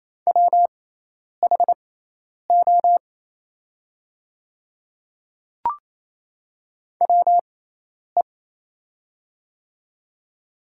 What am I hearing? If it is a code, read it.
Morse code, heard as a keyed tone, about 28 wpm, spaced as for 10 wpm, 700 Hz. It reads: WHO WE
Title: 6x Character Spacing - Top 100 Words - 28wpm